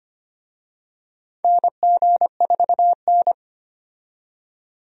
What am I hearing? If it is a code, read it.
Morse code, heard as a keyed tone, about 25 wpm, 700 Hz. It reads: NG4N
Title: USA Call Signs - 4 Characters - 25wpm